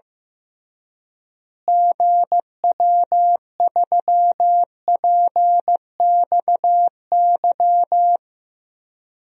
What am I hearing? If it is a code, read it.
Morse code, heard as a keyed tone, about 15 wpm, 700 Hz. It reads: GW3PXY
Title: DXCC-100 Call Signs - 6 Characters - Repeated 3x - 15wpm